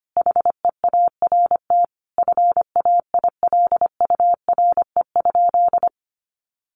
Morse code, heard as a keyed tone, about 25 wpm, 700 Hz. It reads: HEART FAILURE?
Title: Sentences from Top 1200 Words 25wpm